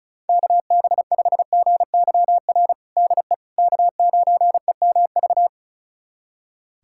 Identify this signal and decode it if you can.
Morse code, heard as a keyed tone, about 36 wpm, 700 Hz. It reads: KB5GYR DE K9EMV
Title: USA Callsign DE Callsign 35wpm